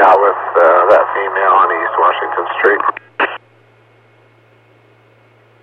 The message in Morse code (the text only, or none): none